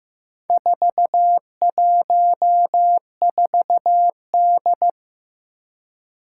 Morse code, heard as a keyed tone, about 15 wpm, 700 Hz. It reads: 414D
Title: Four Challenging Letter-Number Combinations 15wpm